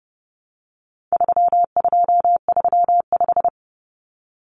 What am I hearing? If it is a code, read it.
Morse code, heard as a keyed tone, about 29 wpm, 700 Hz. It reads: 3235
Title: Numbers - 4-Digit 30wpm